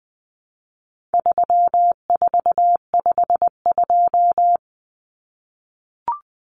Morse code, heard as a keyed tone, about 20 wpm, 700 Hz. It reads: 3452
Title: Numbers - 4-Digit 20wpm